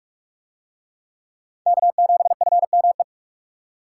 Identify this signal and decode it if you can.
Morse code, heard as a keyed tone, about 39 wpm, 700 Hz. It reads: K7FME